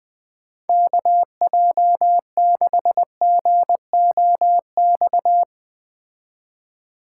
Morse code, heard as a keyed tone, about 21 wpm, 700 Hz. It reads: KJ6GOX